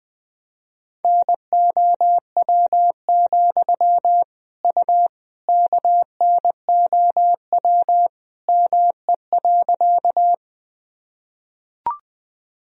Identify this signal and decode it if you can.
Morse code, heard as a keyed tone, about 20 wpm, 700 Hz. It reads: NOW, U KNOW ME.
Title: Sentences from Top 200 Words and Abbreviations 20wpm